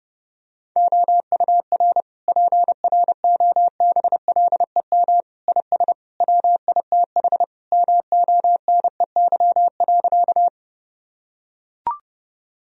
Morse code, heard as a keyed tone, about 30 wpm, 700 Hz. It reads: OUR PROBLEM IS WITH MONEY.
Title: Sentences from Top 1000 Words - Review Part 2 - 30wpm